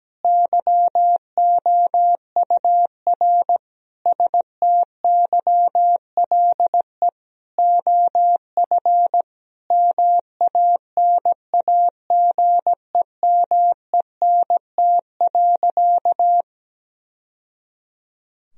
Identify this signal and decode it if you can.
Morse code, heard as a keyed tone, about 17 wpm, 700 Hz. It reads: YOUR STYLE OF MANAGEMENT.
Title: Sentences from Top 1000 Words - Review Part 4 - 17wpm